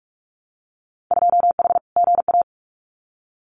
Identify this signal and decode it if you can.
Morse code, heard as a keyed tone, about 46 wpm, 700 Hz. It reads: 2H GA
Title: Winter Field Day Exchanges 45wpm